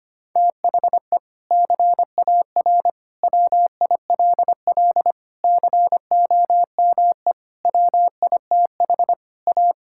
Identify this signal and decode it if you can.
Morse code, heard as a keyed tone, about 25 wpm, 700 Hz. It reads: THE CAR WILL COME WITH A
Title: Sentences from Top 1600 Words 25wpm